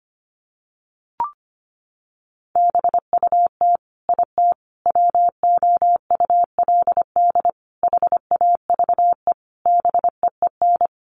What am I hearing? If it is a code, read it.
Morse code, heard as a keyed tone, about 25 wpm, 700 Hz. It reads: BUT IT WOULD HAVE BEEN
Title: Sentences from Top 1500 Words 25wpm